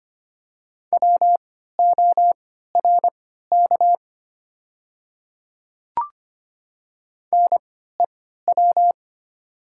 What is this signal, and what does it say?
Morse code, heard as a keyed tone, about 25 wpm, spaced as for 14 wpm, 700 Hz. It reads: WORK NEW
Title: 3x Character Spacing - Top 200 Words - 25wpm